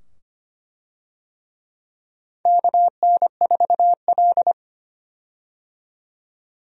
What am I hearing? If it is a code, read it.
Morse code, heard as a keyed tone, about 25 wpm, 700 Hz. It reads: KN4L
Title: USA Call Signs - 4 Characters - 25wpm